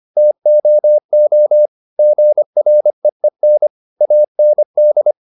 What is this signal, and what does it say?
Morse code, heard as a keyed tone, about 25 wpm, 600 Hz. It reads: TOO GREEN AND
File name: the old man and the sea part 1 25wpm